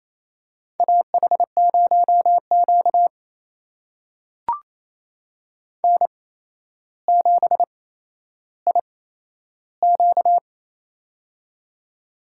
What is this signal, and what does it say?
Morse code, heard as a keyed tone, about 27 wpm, spaced as for 12 wpm, 700 Hz. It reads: AH0Q N7IQ
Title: USA Call Signs - 4 Characters - 8x Character Spacing - 28wpm